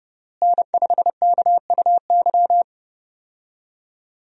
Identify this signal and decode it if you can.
Morse code, heard as a keyed tone, about 31 wpm, 700 Hz. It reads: N5KUY
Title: USA Call Signs - 5 Characters - Repeated 3x - 30wpm